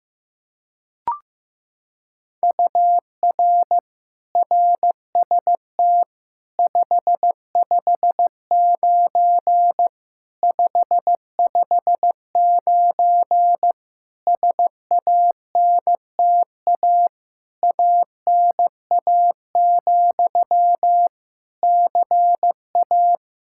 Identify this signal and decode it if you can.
Morse code, heard as a keyed tone, about 15 wpm, 700 Hz. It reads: UR RST 559 559 SANTA ANA, CA
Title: Practice QSO - Exchange 1 - Beginning - 15wpm